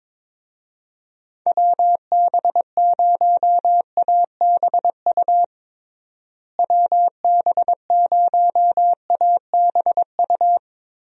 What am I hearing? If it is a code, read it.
Morse code, heard as a keyed tone, about 22 wpm, 700 Hz. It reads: WB0ABU WB0ABU
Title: USA Call Signs - 6 Characters - Repeated 3x - 22wpm